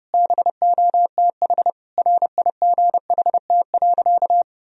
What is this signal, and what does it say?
Morse code, heard as a keyed tone, about 30 wpm, 700 Hz. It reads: BOTH RIGHT.
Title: Sentences from Top 1000 Words - Review Part 1 - 30wpm